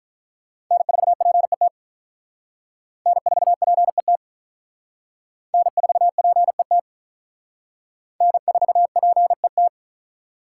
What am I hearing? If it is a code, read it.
Morse code, heard as a keyed tone, about 46 wpm, 700 Hz. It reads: N4PET N4PET N4PET N4PET